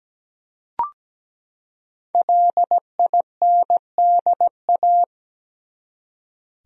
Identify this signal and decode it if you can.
Morse code, heard as a keyed tone, about 17 wpm, 700 Hz. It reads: LINDA
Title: USA Common Names - Max 5 Letters - 17wpm